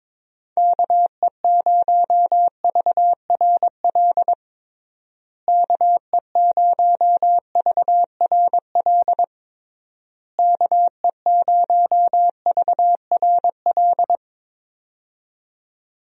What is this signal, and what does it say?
Morse code, heard as a keyed tone, about 22 wpm, 700 Hz. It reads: KE0VRL KE0VRL KE0VRL
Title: USA Call Signs - 6 Characters - Repeated 3x - 22wpm